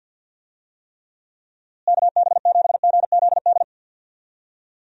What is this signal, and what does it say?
Morse code, heard as a keyed tone, about 51 wpm, 700 Hz. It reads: KB7GZD
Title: USA Call Signs - 6 Characters - Repeated 2x - 50wpm